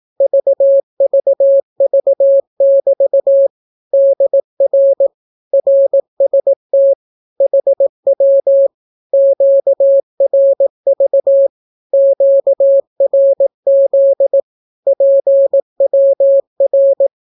Morse code, heard as a keyed tone, about 18 wpm, 550 Hz. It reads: VVV<BT> DR RST HW QRV QRZ PWR
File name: naukapodstaw / skroty / skroty